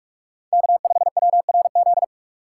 Morse code, heard as a keyed tone, about 46 wpm, 700 Hz. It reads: K5WRZ
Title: USA Call Signs - 5 Characters - 3x Character Spacing - 45wpm